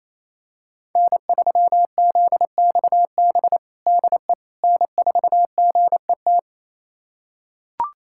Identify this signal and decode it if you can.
Morse code, heard as a keyed tone, about 28 wpm, 700 Hz. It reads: N3ZXB DE N4GET
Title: USA Callsign DE Callsign - 5 Characters - 28wpm